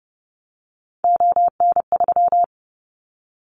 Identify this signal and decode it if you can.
Morse code, heard as a keyed tone, about 32 wpm, 700 Hz. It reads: ON3
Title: Callsign Prefixes for Top 50 DXCC - 30wpm